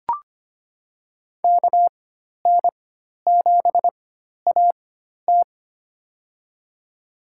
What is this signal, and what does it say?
Morse code, heard as a keyed tone, about 25 wpm, spaced as for 12 wpm, 700 Hz. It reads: KN7AT
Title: USA Call Signs - 5 Characters - 4x Character Spacing - 25wpm